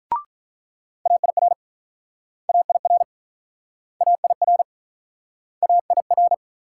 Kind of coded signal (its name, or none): Morse code